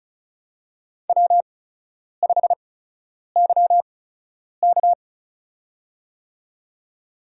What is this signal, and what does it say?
Morse code, heard as a keyed tone, about 36 wpm, spaced as for 10 wpm, 700 Hz. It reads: W5YK
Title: USA Call Signs - 4 Characters - 8x Character Spacing - 35wpm